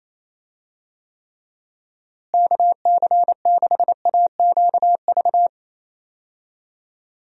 Morse code, heard as a keyed tone, about 28 wpm, 700 Hz. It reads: KC6AQV
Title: USA Call Signs - 6 Characters - Repeated 2x - 28wpm